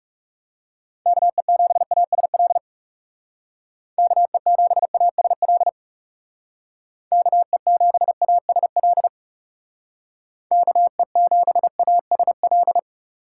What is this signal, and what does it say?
Morse code, heard as a keyed tone, about 39 wpm, 700 Hz. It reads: KE7ASL KE7ASL KE7ASL KE7ASL